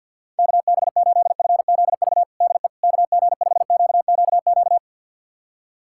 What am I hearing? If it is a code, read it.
Morse code, heard as a keyed tone, about 49 wpm, 700 Hz. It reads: KB8FZV DE KG5QQQ
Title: USA Callsign DE Callsign - 6 Characters - 50wpm